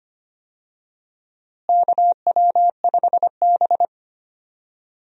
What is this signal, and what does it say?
Morse code, heard as a keyed tone, about 25 wpm, 700 Hz. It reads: KW5B
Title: USA Call Signs - 4 Characters - Repeated 3x - 25wpm